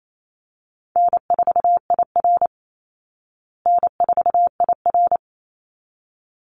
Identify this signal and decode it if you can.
Morse code, heard as a keyed tone, about 28 wpm, 700 Hz. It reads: N4IR N4IR
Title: Call Signs Repeated 3x - 28wpm